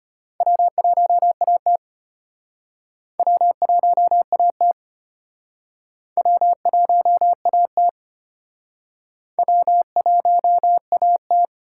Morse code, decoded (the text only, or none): W1AT W1AT W1AT W1AT